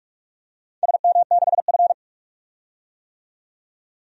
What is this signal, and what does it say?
Morse code, heard as a keyed tone, about 46 wpm, 700 Hz. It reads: SM6F
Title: DXCC-100 Call Signs - 4 Characters - 45wpm